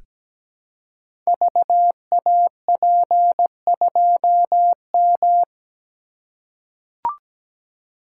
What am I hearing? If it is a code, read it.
Morse code, heard as a keyed tone, about 17 wpm, 700 Hz. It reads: VAP2M
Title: Code Groups - 5 Letter:Numbers - 17wpm